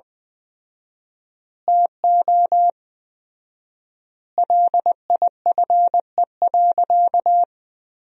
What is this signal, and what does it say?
Morse code, heard as a keyed tone, about 20 wpm, 700 Hz. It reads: TO LIFE.
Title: 4x Word-Spacing - Sentences from Top 400 Words - 20wpm